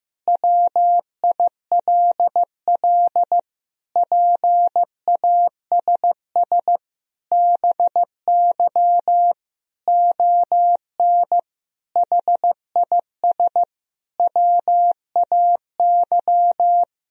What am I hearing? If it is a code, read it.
Morse code, heard as a keyed tone, about 15 wpm, 700 Hz. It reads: WILL PASS BY ON HIS WAY
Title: Sentences from Top 1000 Words - Review Part 1 - 15wpm